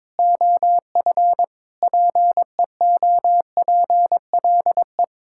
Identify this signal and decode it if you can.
Morse code, heard as a keyed tone, about 22 wpm, 700 Hz. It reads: OF PEOPLE
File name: Sentences with 7-Word Phrases - 22wpm